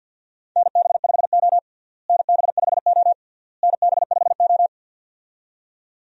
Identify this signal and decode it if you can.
Morse code, heard as a keyed tone, about 53 wpm, 700 Hz. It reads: NB5O NB5O NB5O
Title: USA Call Signs - 4 Characters - Repeated 3x - 50wpm